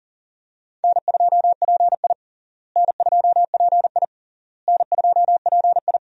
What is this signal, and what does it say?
Morse code, heard as a keyed tone, about 40 wpm, 700 Hz. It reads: N2PI N2PI N2PI
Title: USA Call Signs - 4 Characters - Repeated 3x - 40wpm